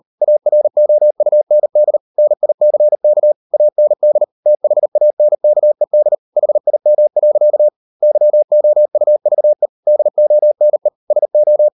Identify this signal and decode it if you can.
Morse code, heard as a keyed tone, about 40 wpm, 600 Hz. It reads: AROUND NICK AND THANKED HIM. YOUVE DONE SO